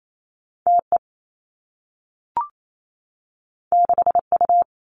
Morse code, heard as a keyed tone, about 28 wpm, 700 Hz. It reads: T E 6 U
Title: Two Challenging Letter-Number Combinations 28wpm